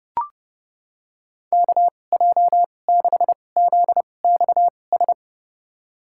Morse code, heard as a keyed tone, about 32 wpm, spaced as for 22 wpm, 700 Hz. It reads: KJ6ZXS
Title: USA Call Signs - 2x Character Spacing - 30wpm